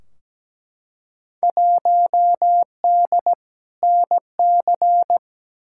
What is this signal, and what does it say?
Morse code, heard as a keyed tone, about 17 wpm, 700 Hz. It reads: 1D NC
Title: Field Day Exchanges - 17wpm